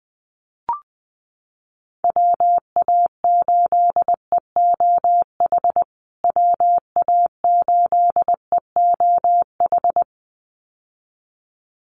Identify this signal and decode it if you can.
Morse code, heard as a keyed tone, about 20 wpm, 700 Hz. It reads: WA8EOH WA8EOH
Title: USA Call Signs Repeated 2x 20wpm